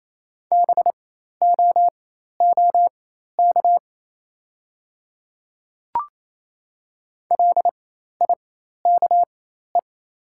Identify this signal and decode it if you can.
Morse code, heard as a keyed tone, about 28 wpm, spaced as for 13 wpm, 700 Hz. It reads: BOOK LIKE
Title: 4x Character Spacing - Top 300 Words - 28wpm